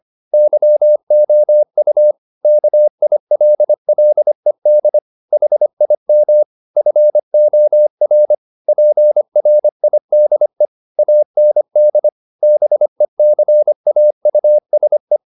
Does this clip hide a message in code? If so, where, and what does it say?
Yes: Morse code from 0.3 to 15.2 s, heard as a keyed tone, about 25 wpm, 600 Hz. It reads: YOU KILLED HIM FOR PRIDE AND BECAUSE